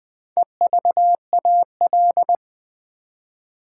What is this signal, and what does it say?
Morse code, heard as a keyed tone, about 20 wpm, 700 Hz. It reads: EVAL
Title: Build A Word - 4 Letters At A Time - 10-Letter Words - 20wpm